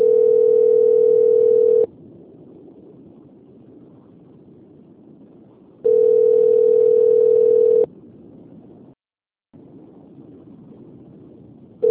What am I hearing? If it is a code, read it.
Ringback tone (North America), 440 + 480 Hz.
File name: out- - - - - .8053